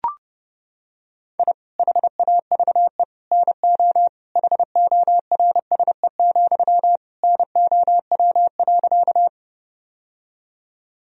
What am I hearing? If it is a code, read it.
Morse code, heard as a keyed tone, about 30 wpm, 700 Hz. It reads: I HAVE NO HORSE, NOW.
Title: Sentences from Top 1000 Words - Review Part 1 - 30wpm